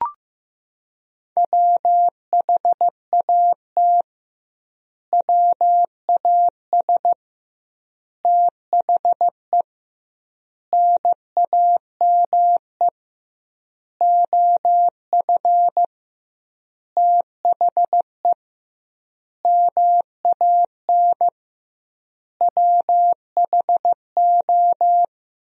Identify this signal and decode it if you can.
Morse code, heard as a keyed tone, about 16 wpm, 700 Hz. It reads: WHAT WAS THE NAME OF THE MAN WHO
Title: 2x Word-Spacing - Sentences from Top 300 Words - 15wpm